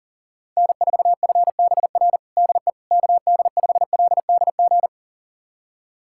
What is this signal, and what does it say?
Morse code, heard as a keyed tone, about 40 wpm, 700 Hz. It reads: N4FBR DE KD5LDG
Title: USA Callsign DE Callsign 40wpm